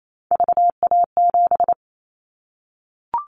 Morse code, heard as a keyed tone, about 28 wpm, 700 Hz. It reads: VA7
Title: Callsign Prefixes for Top 50 DXCC - 28wpm